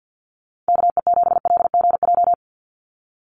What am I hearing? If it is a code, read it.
Morse code, heard as a keyed tone, about 48 wpm, 700 Hz. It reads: KE7LGJ
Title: USA Call Signs 50wpm